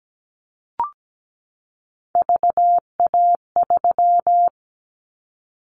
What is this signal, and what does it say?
Morse code, heard as a keyed tone, about 17 wpm, 700 Hz. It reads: VA3
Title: Callsign Prefixes for Top 100 DXCC - 17wpm